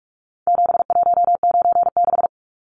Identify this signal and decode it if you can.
Morse code, heard as a keyed tone, about 44 wpm, 700 Hz. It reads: 7196
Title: Numbers - 4-Digit 45wpm